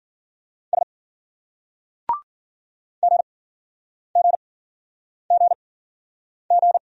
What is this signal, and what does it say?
Morse code, heard as a keyed tone, about 48 wpm, spaced as for 9 wpm, 700 Hz. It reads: S GGGG
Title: Speed-Racing Single Letters 40wpm